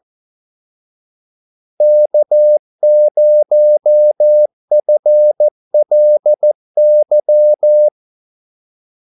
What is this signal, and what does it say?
Morse code, heard as a keyed tone, about 14 wpm, 600 Hz. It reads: K0FLY
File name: Callsigns-14wpm